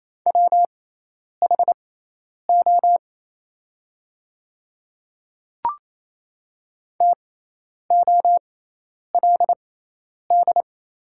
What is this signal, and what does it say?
Morse code, heard as a keyed tone, about 28 wpm, spaced as for 10 wpm, 700 Hz. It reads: WHO TOLD
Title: 6x Character Spacing - Top 500 Words - 28wpm